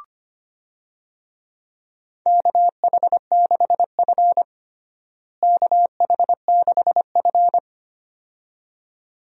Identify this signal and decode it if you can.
Morse code, heard as a keyed tone, about 25 wpm, 700 Hz. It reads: KH6F KH6F
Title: USA Call Signs - 4 Characters - Repeated 2x - 25wpm